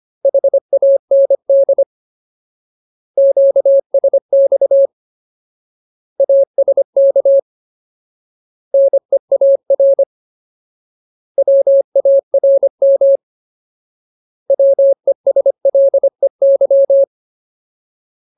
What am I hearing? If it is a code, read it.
Morse code, heard as a keyed tone, about 25 wpm, 550 Hz. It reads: HAND QSX ASK NEAR WARM WESLEY
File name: Ultimate Word List / 25WPM Ultimate